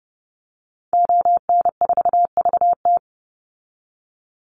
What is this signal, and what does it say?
Morse code, heard as a keyed tone, about 30 wpm, 700 Hz. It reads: ON4VT
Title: Call Signs Repeated 3x - 30wpm